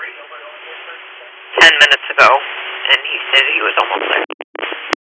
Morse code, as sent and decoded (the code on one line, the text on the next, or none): none